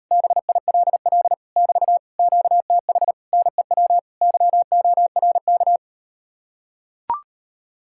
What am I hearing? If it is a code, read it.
Morse code, heard as a keyed tone, about 39 wpm, 700 Hz. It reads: BILL <BT> QTH NEW YORK